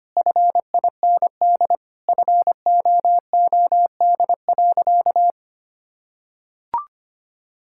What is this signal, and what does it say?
Morse code, heard as a keyed tone, about 25 wpm, 700 Hz. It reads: FIND FOOD.
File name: practice / 25wpm / Sentences with 3-Word Phrases - 25wpm